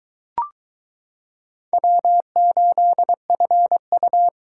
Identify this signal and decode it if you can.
Morse code, heard as a keyed tone, about 22 wpm, 700 Hz. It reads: W8FU